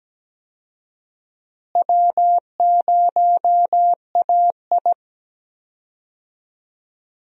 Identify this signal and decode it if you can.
Morse code, heard as a keyed tone, about 17 wpm, 700 Hz. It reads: W0AI
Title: USA Call Signs - 4 Characters - Repeated 2x - 17wpm